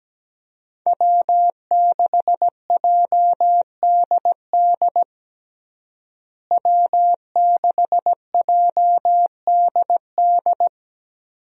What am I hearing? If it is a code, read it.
Morse code, heard as a keyed tone, about 17 wpm, 700 Hz. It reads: W6JDD W6JDD
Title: USA Call Signs - 5 Characters - Repeated 2x - 17wpm